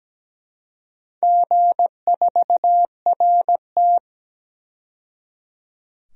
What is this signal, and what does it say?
Morse code, heard as a keyed tone, about 17 wpm, 700 Hz. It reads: G4RT
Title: DXCC-100 Call Signs - 4 Characters - 17wpm